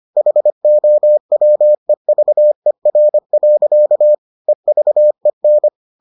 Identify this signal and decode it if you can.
Morse code, heard as a keyed tone, about 25 wpm, 600 Hz. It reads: HOWEVER. EVEN